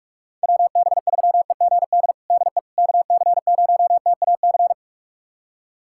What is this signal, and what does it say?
Morse code, heard as a keyed tone, about 42 wpm, 700 Hz. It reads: WB3EGD DE KC0TAC
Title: USA Callsign DE Callsign - 6 Characters - 45wpm